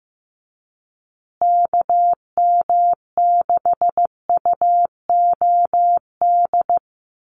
Morse code, heard as a keyed tone, about 15 wpm, 700 Hz. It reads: KM6UOD
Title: USA Call Signs Repeated 2x 15wpm